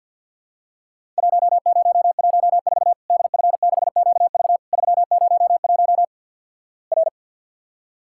Morse code, heard as a keyed tone, about 50 wpm, 700 Hz. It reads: 1014 DF6QV 301 R
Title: Worked All Europe DX Contest - QTC Exchange - 50wpm